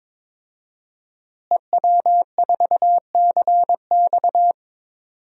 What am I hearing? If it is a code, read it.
Morse code, heard as a keyed tone, about 22 wpm, 700 Hz. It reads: EW4CX
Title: DXCC-100 Call Signs - 5 Characters - Repeated 2x - 22wpm